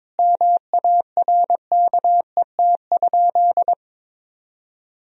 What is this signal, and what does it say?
Morse code, heard as a keyed tone, about 22 wpm, 700 Hz. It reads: MARKET?
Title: Sentences from Top 1000 Words - Review Part 3 - 22wpm